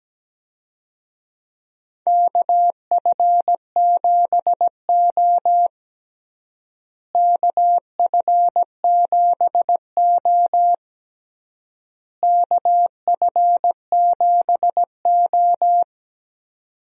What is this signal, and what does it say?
Morse code, heard as a keyed tone, about 17 wpm, 700 Hz. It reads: KF7O KF7O KF7O
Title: USA Call Signs - 4 Characters - Repeated 3x - 17wpm